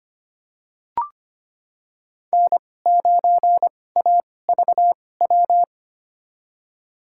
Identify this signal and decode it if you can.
Morse code, heard as a keyed tone, about 25 wpm, spaced as for 18 wpm, 700 Hz. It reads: N9AVW
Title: USA Call Signs - 5 Characters - 2x Character Spacing - 25wpm